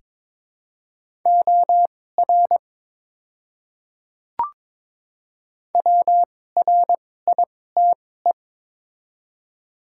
Morse code, heard as a keyed tone, about 22 wpm, spaced as for 16 wpm, 700 Hz. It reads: OR WRITE
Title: 2x Character Spacing - Top 100 Words - 22wpm